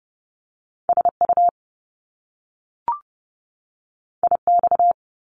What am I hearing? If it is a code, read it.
Morse code, heard as a keyed tone, about 30 wpm, 700 Hz. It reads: SU IX
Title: Top Two Letter Combinations in Words 30wpm